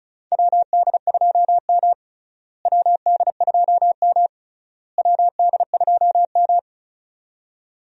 Morse code, heard as a keyed tone, about 36 wpm, 700 Hz. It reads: WD2M WD2M WD2M
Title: USA Call Signs - 4 Characters - Repeated 3x - 35wpm